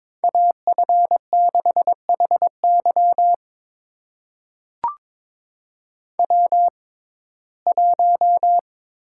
Morse code, heard as a keyed tone, about 22 wpm, spaced as for 16 wpm, 700 Hz. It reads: AF6HY W1
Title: USA Call Signs - 5 Characters - 6x Character Spacing - 22wpm